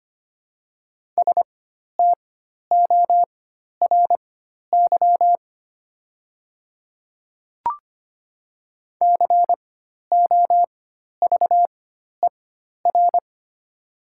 Morse code, heard as a keyed tone, about 25 wpm, spaced as for 12 wpm, 700 Hz. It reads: STORY COVER
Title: 4x Character Spacing - Top 300 Words - 25wpm